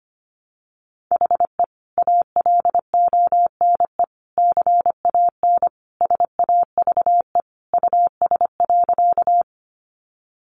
Morse code, heard as a keyed tone, about 25 wpm, 690 Hz. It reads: HE ALONE CAN SAVE US.
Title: Sentences from Top 1100 Words 25wpm